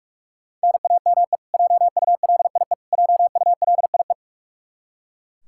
Morse code, heard as a keyed tone, about 42 wpm, 700 Hz. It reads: NAME JULIE JULIE
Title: Practice QSO - Exchange 1 - Beginning - 45wpm